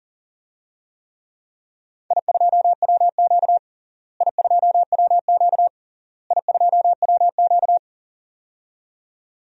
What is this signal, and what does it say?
Morse code, heard as a keyed tone, about 37 wpm, 700 Hz. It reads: I2WQ I2WQ I2WQ
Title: DXCC-100 Call Signs - 4 Characters - Repeated 3x - 40wpm